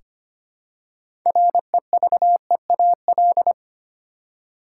Morse code, heard as a keyed tone, about 25 wpm, 700 Hz. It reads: REVEAL
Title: Six Letter Words - 25wpm